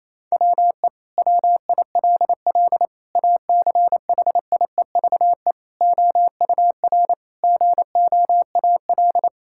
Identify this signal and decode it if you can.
Morse code, heard as a keyed tone, about 28 wpm, 700 Hz. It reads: WE WILL ACHIEVE OUR GOAL
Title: Sentences from Top 1000 Words - Review Part 6 - 28wpm